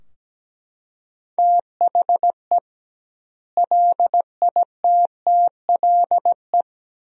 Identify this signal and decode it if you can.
Morse code, heard as a keyed tone, about 17 wpm, 700 Hz. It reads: THE LITTLE
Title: 2x Word-Spacing - Sentences from Top 500 Words - 17wpm